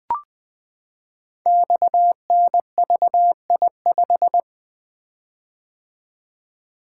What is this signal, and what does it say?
Morse code, heard as a keyed tone, about 20 wpm, 700 Hz. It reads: XNVI5
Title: Code Groups - 5 Letter:Numbers - 20wpm